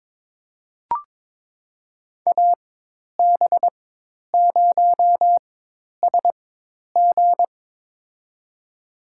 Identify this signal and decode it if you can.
Morse code, heard as a keyed tone, about 22 wpm, spaced as for 10 wpm, 700 Hz. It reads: AB0SG
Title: USA Call Signs - 5 Characters - 4x Character Spacing - 22wpm